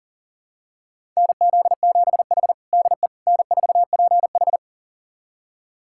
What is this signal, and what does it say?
Morse code, heard as a keyed tone, about 37 wpm, 700 Hz. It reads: NZ7H DE N4PH
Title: USA Callsign DE Callsign - 4 Characters - 40wpm